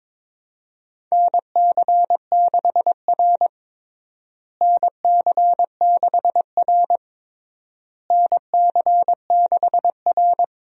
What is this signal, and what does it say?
Morse code, heard as a keyed tone, about 22 wpm, 700 Hz. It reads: NC6R NC6R NC6R
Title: USA Call Signs - 4 Characters - Repeated 3x - 22wpm